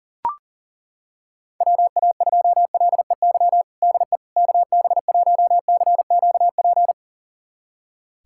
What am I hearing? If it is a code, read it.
Morse code, heard as a keyed tone, about 37 wpm, 700 Hz. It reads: WA2LEY DE KB1CQP
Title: USA Callsign DE Callsign 40wpm